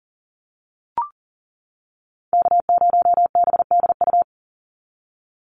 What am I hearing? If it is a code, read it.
Morse code, heard as a keyed tone, about 40 wpm, 700 Hz. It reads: K0BDU
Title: USA Call Signs 40wpm